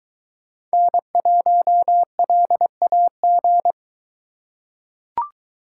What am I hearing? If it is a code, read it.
Morse code, heard as a keyed tone, about 22 wpm, 700 Hz. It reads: N1LAG